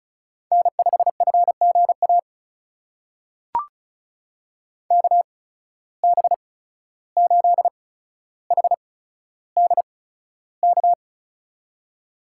Morse code, heard as a keyed tone, about 36 wpm, spaced as for 14 wpm, 700 Hz. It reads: N5FGA KB8HDK
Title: USA Call Signs - 8x Character Spacing - 35wpm